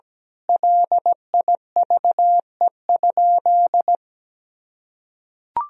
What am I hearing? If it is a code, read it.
Morse code, heard as a keyed tone, about 17 wpm, 700 Hz. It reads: LIVE?
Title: Sentences from Top 1000 Words - Review Part 4 - 17wpm